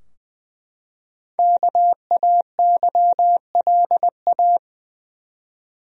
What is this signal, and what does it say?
Morse code, heard as a keyed tone, about 20 wpm, 700 Hz. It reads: KAYLA
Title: USA Common Names - Max 5 Letters - 20wpm